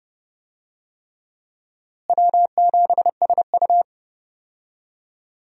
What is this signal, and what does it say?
Morse code, heard as a keyed tone, about 30 wpm, 700 Hz. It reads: W7SU
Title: USA Call Signs - 4 Characters - Repeated 3x - 30wpm